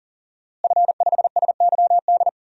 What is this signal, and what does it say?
Morse code, heard as a keyed tone, about 37 wpm, 700 Hz. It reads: F5SYD
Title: DXCC-100 Call Signs - 5 Characters - Repeated 2x - 40wpm